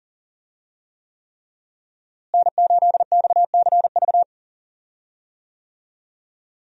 Morse code, heard as a keyed tone, about 40 wpm, 700 Hz. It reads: N8XCV
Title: USA Call Signs - 5 Characters - Repeated 3x - 40wpm